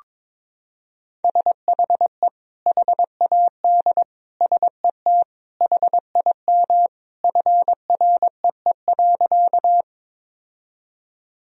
Morse code, heard as a keyed tone, about 22 wpm, 700 Hz. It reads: SHE HAD SET HIM FREE.
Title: Sentences from Top 1000 Words - Review Part 1 - 22wpm